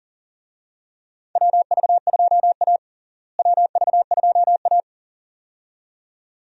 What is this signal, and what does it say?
Morse code, heard as a keyed tone, about 40 wpm, 700 Hz. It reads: WV2A WV2A
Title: USA Call Signs - 4 Characters - Repeated 2x - 40wpm